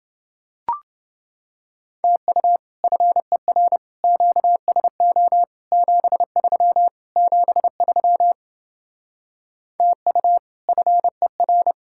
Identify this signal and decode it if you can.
Morse code, heard as a keyed tone, about 30 wpm, 700 Hz. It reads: TU FER QSO 73 73 TU FER